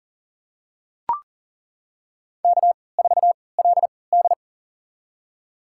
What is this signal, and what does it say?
Morse code, heard as a keyed tone, about 39 wpm, spaced as for 22 wpm, 700 Hz. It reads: K4LD
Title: USA Call Signs - 4 Characters - 3x Character Spacing - 40wpm